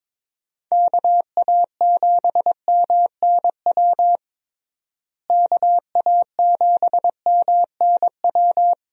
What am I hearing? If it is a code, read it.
Morse code, heard as a keyed tone, about 22 wpm, 700 Hz. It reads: KA7MNW KA7MNW
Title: USA Call Signs - 6 Characters - Repeated 3x - 22wpm